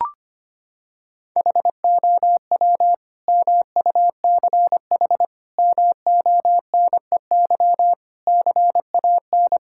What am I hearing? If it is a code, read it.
Morse code, heard as a keyed tone, about 25 wpm, 700 Hz. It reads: HOW MUCH MONEY CAN
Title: Sentences from Top 1000 Words - Review Part 5 - 25wpm